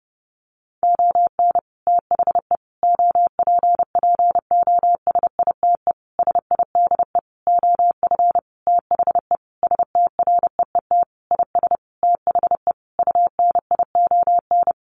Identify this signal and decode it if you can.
Morse code, heard as a keyed tone, about 30 wpm, 700 Hz. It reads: ON THE OPPOSITE SIDE OF THE STREET IS THE UNION
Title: Sentences from Top 1400 Words 30wpm